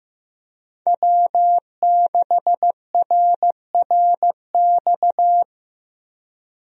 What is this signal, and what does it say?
Morse code, heard as a keyed tone, about 15 wpm, 700 Hz. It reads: W6RRX
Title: USA Call Signs - 5 Characters - Repeated 2x - 15wpm